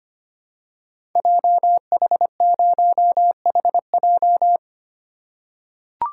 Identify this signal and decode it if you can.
Morse code, heard as a keyed tone, about 25 wpm, 690 Hz. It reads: JH0HJ
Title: Code Groups - 5 Letter:Numbers - 25wpm